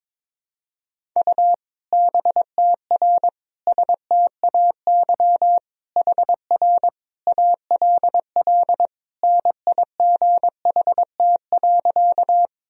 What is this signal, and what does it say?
Morse code, heard as a keyed tone, about 22 wpm, 700 Hz. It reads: U BTR STAY HR ALL NIGHT.
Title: Sentences from Top 500 Words and Abbreviations 22wpm